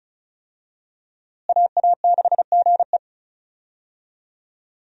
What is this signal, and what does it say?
Morse code, heard as a keyed tone, about 36 wpm, 700 Hz. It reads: AA6GE
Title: USA Call Signs - 5 Characters - Repeated 3x - 35wpm